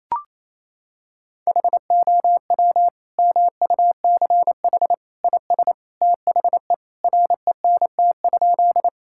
Morse code, heard as a keyed tone, about 28 wpm, 700 Hz. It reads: HOW MUCH IS THE RENT?
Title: Sentences from Top 1000 Words - Review Part 5 - 28wpm